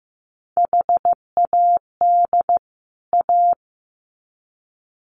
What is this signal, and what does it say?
Morse code, heard as a keyed tone, about 15 wpm, 700 Hz. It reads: HAD A
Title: Sets of 2 Words 15wpm